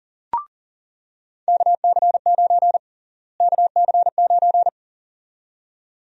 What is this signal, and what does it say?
Morse code, heard as a keyed tone, about 37 wpm, 700 Hz. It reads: KC9 KC9
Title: USA Call Sign Prefixes - Repeated 2x - 40wpm